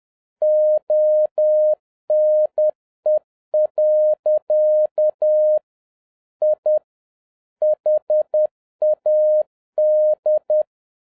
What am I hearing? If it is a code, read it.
Morse code, heard as a keyed tone, about 10 wpm, 610 Hz. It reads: ONE. I HAD